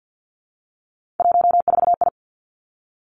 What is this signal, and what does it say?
Morse code, heard as a keyed tone, about 50 wpm, 700 Hz. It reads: 14I
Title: Three Challenging Letter-Number Combinations 50wpm